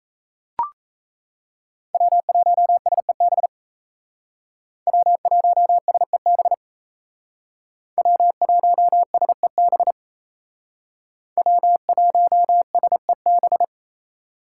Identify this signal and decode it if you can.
Morse code, heard as a keyed tone, about 36 wpm, 700 Hz. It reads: W1SEB W1SEB W1SEB W1SEB